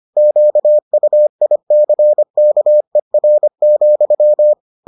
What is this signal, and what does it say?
Morse code, heard as a keyed tone, about 25 wpm, 600 Hz. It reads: QUICKER,